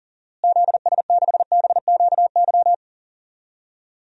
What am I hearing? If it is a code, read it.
Morse code, heard as a keyed tone, about 40 wpm, 700 Hz. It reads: ZS6BQY
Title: DXCC-100 Call Signs - 6 Characters - Repeated 3x - 40wpm